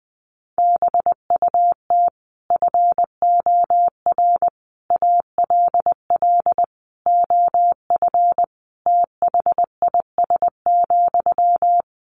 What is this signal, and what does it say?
Morse code, heard as a keyed tone, about 20 wpm, 700 Hz. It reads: BUT FOR ALL OF THIS,
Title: Sentences from Top 1300 Words 20wpm